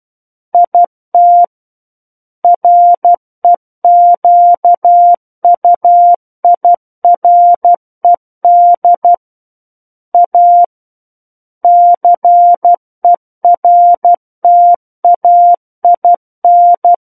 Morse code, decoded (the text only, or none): IT REQUIRED A CERTAIN